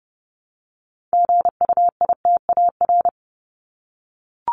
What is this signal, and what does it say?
Morse code, heard as a keyed tone, about 30 wpm, 700 Hz. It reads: GUITAR
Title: Top 1501-1750 Words 30wpm